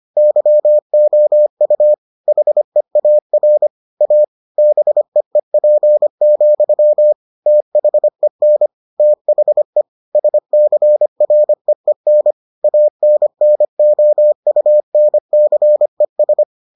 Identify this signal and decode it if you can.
Morse code, heard as a keyed tone, about 25 wpm, 600 Hz. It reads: YOU HEAR A BEEP, THEN THE SCREEN ANNOUNCES